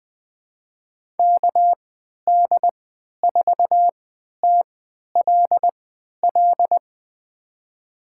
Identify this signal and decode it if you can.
Morse code, heard as a keyed tone, about 20 wpm, spaced as for 11 wpm, 700 Hz. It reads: KD4TLL
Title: USA Call Signs - 6 Characters - 3x Character Spacing - 20wpm